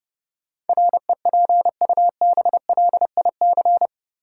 Morse code, heard as a keyed tone, about 30 wpm, 700 Hz. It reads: REPUBLIC
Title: Sentences from Top 1500 Words - New Word Drill - 30wpm